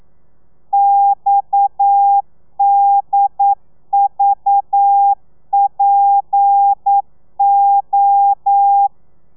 Morse code, heard as a keyed tone, about 9 wpm, 800 Hz. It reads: XDVPO
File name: telegrafia / i2tfj / Lez7 40